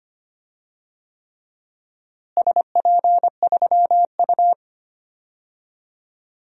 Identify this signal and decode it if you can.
Morse code, heard as a keyed tone, about 25 wpm, 700 Hz. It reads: SP3U
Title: DXCC-100 Call Signs - 4 Characters - Repeated 3x - 25wpm